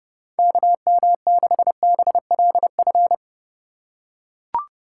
Morse code, heard as a keyed tone, about 30 wpm, 700 Hz. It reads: KM6BLF